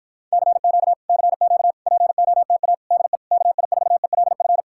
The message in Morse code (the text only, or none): CQ CQ POTA DE KI4ELF